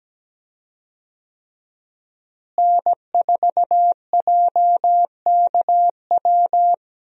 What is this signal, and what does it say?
Morse code, heard as a keyed tone, about 17 wpm, 700 Hz. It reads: N4JKW
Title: USA Call Signs - 5 Characters - Repeated 3x - 17wpm